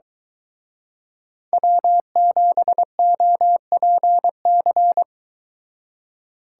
Morse code, heard as a keyed tone, about 23 wpm, 700 Hz. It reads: W7OPC